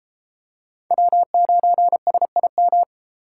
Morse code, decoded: W9SIM

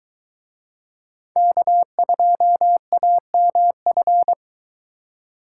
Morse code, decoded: K2AMF

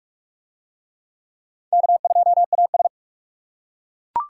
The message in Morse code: K2AS

